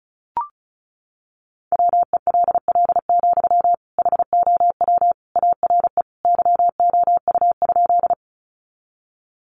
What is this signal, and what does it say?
Morse code, heard as a keyed tone, about 34 wpm, 700 Hz. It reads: WELL, HOW ARE YOU?
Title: Sentences from Top 200 Words 35wpm